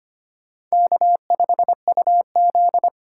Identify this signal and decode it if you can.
Morse code, heard as a keyed tone, about 25 wpm, 700 Hz. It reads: K5UZ
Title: USA Call Signs - 4 Characters - 3x Character Spacing - 25wpm